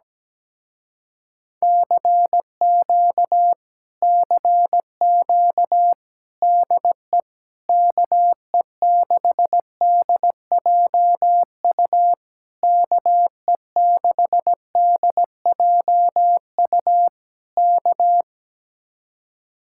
Morse code, heard as a keyed tone, about 17 wpm, 700 Hz. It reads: CQ CQ DE KE6DJU KE6DJU K
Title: Speed-Racing Common QSO Phrases - 17wpm